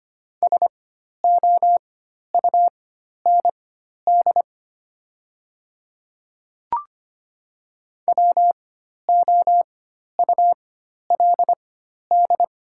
Morse code, heard as a keyed tone, about 25 wpm, spaced as for 12 wpm, 700 Hz. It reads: SOUND WOULD
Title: 4x Character Spacing - Top 100 Words - 25wpm